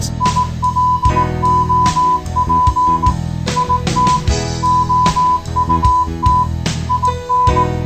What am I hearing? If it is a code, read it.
Morse code, heard as a keyed tone, about 18 wpm, 1000 Hz. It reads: ILOP IT OUT IN